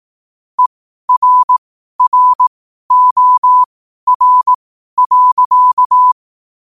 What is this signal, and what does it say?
Morse code, heard as a keyed tone, about 18 wpm, spaced as for 12 wpm, 1000 Hz. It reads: ERROR.
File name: qotdfiles / 12wpm